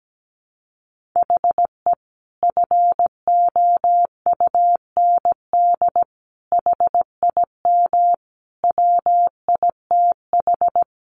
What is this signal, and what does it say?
Morse code, heard as a keyed tone, about 17 wpm, 700 Hz. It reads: HE FOUND HIM WITH
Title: Sentences from Top 1500 Words 17wpm